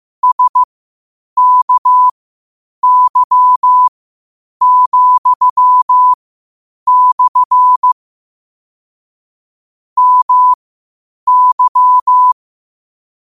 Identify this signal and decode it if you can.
Morse code, heard as a keyed tone, about 15 wpm, spaced as for 8 wpm, 1000 Hz. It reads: SKY,/ MY
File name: qotdfiles / 8wpm